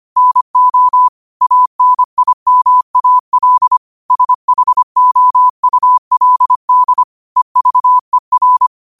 Morse code, heard as a keyed tone, about 25 wpm, 1000 Hz. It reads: NO ANIMAL SHOULD EVER
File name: qotdfiles / 25wpm